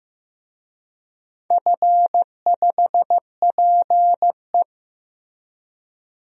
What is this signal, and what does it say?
Morse code, heard as a keyed tone, about 15 wpm, 700 Hz. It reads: F5PE
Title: DXCC-100 Call Signs - 4 Characters - Repeated 2x - 15wpm